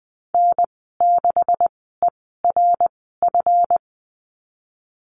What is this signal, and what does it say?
Morse code, heard as a keyed tone, about 20 wpm, spaced as for 15 wpm, 700 Hz. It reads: N6ERF
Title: 186 Lesson 30g - Letter b - Call Signs 20wpm